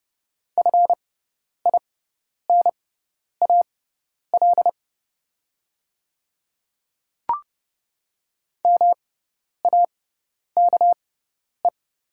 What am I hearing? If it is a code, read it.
Morse code, heard as a keyed tone, about 30 wpm, spaced as for 10 wpm, 700 Hz. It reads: FINAL MAKE
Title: 6x Character Spacing - Top 500 Words - 30wpm